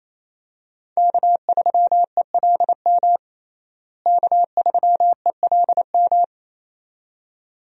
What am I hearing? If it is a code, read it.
Morse code, heard as a keyed tone, about 28 wpm, 700 Hz. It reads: K3ELM K3ELM
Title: USA Call Signs - 5 Characters - Repeated 2x - 28wpm